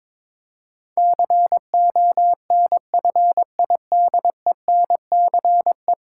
Morse code, heard as a keyed tone, about 22 wpm, 700 Hz. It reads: CONFIDENCE
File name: practice / 22wpm / 10 Letter Words - First 3 Letters Given - 22wpm